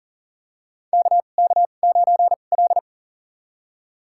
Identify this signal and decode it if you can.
Morse code, heard as a keyed tone, about 40 wpm, spaced as for 29 wpm, 700 Hz. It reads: KK9L
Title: USA Call Signs - 4 Characters - 2x Character Spacing - 40wpm